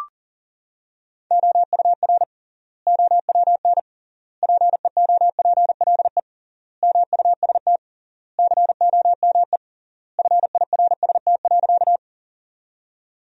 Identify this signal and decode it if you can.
Morse code, heard as a keyed tone, about 37 wpm, 700 Hz. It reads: OUR OWN PEOPLE MUST COME FIRST.
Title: 3x Word-Spacing - Sentences from Top 300 Words - 40wpm